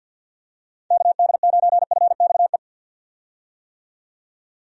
Morse code, heard as a keyed tone, about 51 wpm, 700 Hz. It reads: KD9FXE
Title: USA Call Signs - 6 Characters - 50wpm